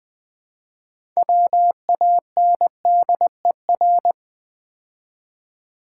Morse code, heard as a keyed tone, about 20 wpm, 700 Hz. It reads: WANDER
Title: Build A Word - 3 Letters At A Time - 6-Letter Words - 20wpm